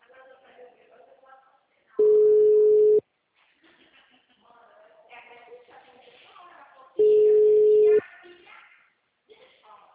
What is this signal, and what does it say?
Ringback tone (Europe, 425 Hz), with speech.